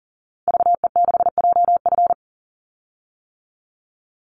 Morse code, heard as a keyed tone, about 40 wpm, 700 Hz. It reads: VE6JF
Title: Call Signs - 40wpm